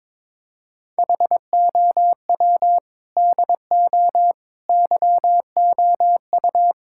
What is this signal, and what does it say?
Morse code, heard as a keyed tone, about 22 wpm, 700 Hz. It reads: HOW DO YOU
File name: practice / 22wpm / Sentences with 4-Word Phrases - 22wpm